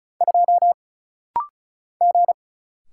Morse code, heard as a keyed tone, about 34 wpm, 700 Hz. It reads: 2 G
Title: Single Letter-Number Alt - Rapid-Fire 35wpm